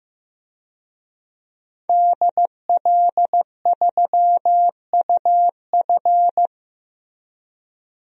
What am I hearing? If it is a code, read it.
Morse code, heard as a keyed tone, about 15 wpm, 700 Hz. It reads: DL3UF
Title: DXCC-100 Call Signs - 5 Characters - 15wpm